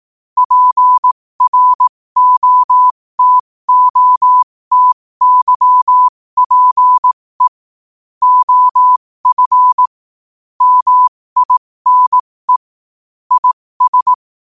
Morse code, heard as a keyed tone, about 18 wpm, spaced as for 15 wpm, 1000 Hz. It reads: PROTOTYPE OF MINE IS
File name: qotdfiles / 15wpm